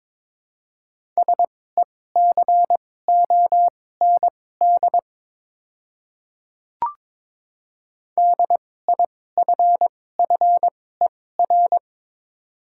Morse code, heard as a keyed tone, about 22 wpm, spaced as for 16 wpm, 700 Hz. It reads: SECOND DIFFER
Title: 2x Character Spacing - Top 300 Words - 22wpm